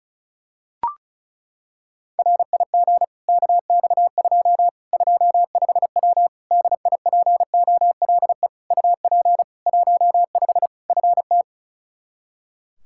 Morse code, heard as a keyed tone, about 34 wpm, 700 Hz. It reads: RIG KX2 25W DIPOLE UP 15 FT
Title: Practice QSO - Exchange 2 - Info - 35wpm